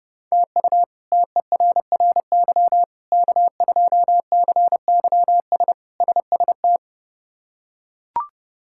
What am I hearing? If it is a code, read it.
Morse code, heard as a keyed tone, about 30 wpm, 700 Hz. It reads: TU TERRY K2CYS SST
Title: K1USN SST QSO Practice - 30wpm